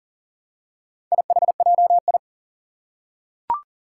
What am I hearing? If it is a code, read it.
Morse code, heard as a keyed tone, about 37 wpm, 700 Hz. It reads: IHJI
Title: Four Challenging Letter-Number Combinations 40wpm